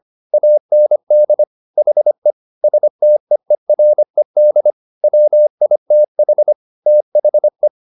Morse code, heard as a keyed tone, about 25 wpm, 600 Hz. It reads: AND HE STEERED WITH THE